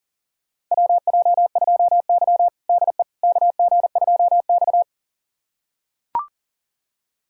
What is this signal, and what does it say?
Morse code, heard as a keyed tone, about 40 wpm, 700 Hz. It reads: WJ2Y DE KG2X
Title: USA Callsign DE Callsign - 4 Characters - 40wpm